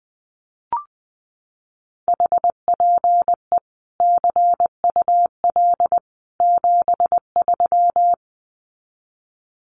Morse code, heard as a keyed tone, about 20 wpm, 700 Hz. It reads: HPE CUL 73
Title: Common QSO Phrases 20wpm